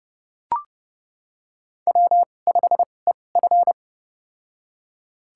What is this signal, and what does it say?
Morse code, heard as a keyed tone, about 30 wpm, spaced as for 22 wpm, 700 Hz. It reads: W5EF
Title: USA Call Signs - 4 Characters - 2x Character Spacing - 30wpm